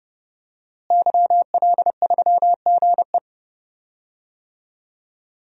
Morse code, heard as a keyed tone, about 31 wpm, 700 Hz. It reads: YL3GE
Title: DXCC-100 Call Signs - 5 Characters - Repeated 3x - 30wpm